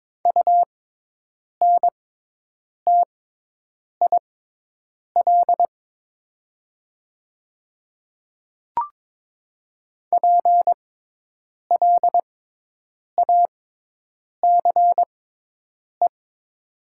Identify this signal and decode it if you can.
Morse code, heard as a keyed tone, about 22 wpm, spaced as for 8 wpm, 700 Hz. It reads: UNTIL PLACE
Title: 6x Character Spacing - Top 500 Words - 22wpm